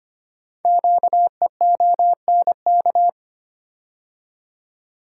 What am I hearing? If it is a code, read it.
Morse code, heard as a keyed tone, about 25 wpm, 700 Hz. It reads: QEONK
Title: Code Groups - 5 Letters - 25wpm